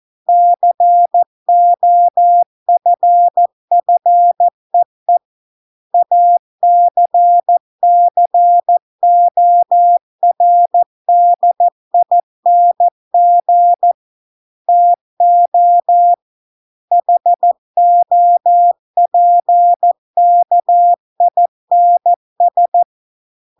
Morse code, heard as a keyed tone, about 14 wpm, 700 Hz. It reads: COFFEE ACCORDING TO HOPKINS